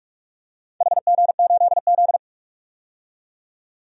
Morse code, heard as a keyed tone, about 46 wpm, 700 Hz. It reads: HG8Z
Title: DXCC-100 Call Signs - 4 Characters - 45wpm